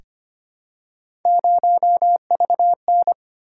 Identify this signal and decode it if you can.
Morse code, heard as a keyed tone, about 25 wpm, 700 Hz. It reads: 0VN
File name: Three Letter-Number Combinations - 25wpm